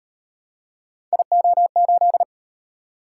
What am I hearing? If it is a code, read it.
Morse code, heard as a keyed tone, about 38 wpm, 700 Hz. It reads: IO8